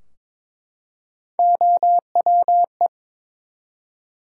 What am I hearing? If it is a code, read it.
Morse code, heard as a keyed tone, about 22 wpm, 700 Hz. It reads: OWE
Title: Three Letter Words - 22wpm